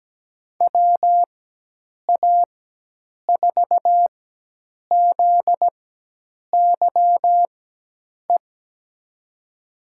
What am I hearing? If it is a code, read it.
Morse code, heard as a keyed tone, about 17 wpm, spaced as for 8 wpm, 700 Hz. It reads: WA4ZYE
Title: USA Call Signs - 6 Characters - 4x Character Spacing - 17wpm